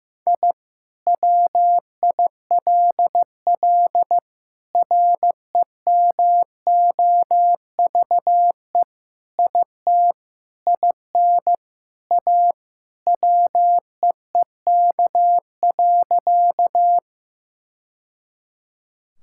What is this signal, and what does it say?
Morse code, heard as a keyed tone, about 16 wpm, 700 Hz. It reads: I WILL REMOVE IT IN A WEEK.
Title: Sentences from Top 1000 Words - Review Part 5 - 15wpm